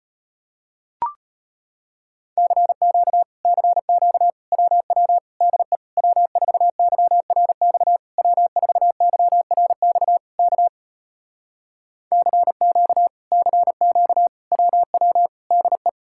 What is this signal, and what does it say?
Morse code, heard as a keyed tone, about 33 wpm, 700 Hz. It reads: CQ CQ WW DE W4YRX W4YRX K CQ CQ WW DE